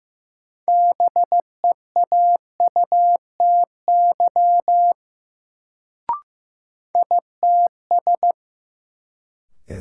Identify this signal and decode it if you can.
Morse code, heard as a keyed tone, about 15 wpm, 700 Hz. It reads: BEAUTY ITS
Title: Sentences from Top 600 Words - New Word Drill - 15wpm